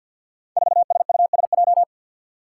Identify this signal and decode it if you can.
Morse code, heard as a keyed tone, about 52 wpm, 700 Hz. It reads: 4SUSJ
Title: Five Challenging Letter-Number Combinations 50wpm